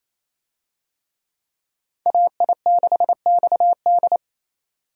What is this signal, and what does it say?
Morse code, heard as a keyed tone, about 28 wpm, 700 Hz. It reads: AI6XD
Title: USA Call Signs - 5 Characters - 28wpm